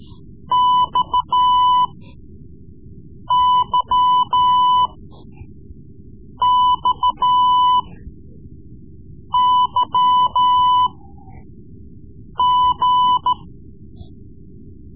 Morse code, heard as a keyed tone, about 14 wpm, spaced as for 5 wpm, 1000 Hz. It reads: XYXYG
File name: Morse Lessons / Morse Lessons / Tape-3B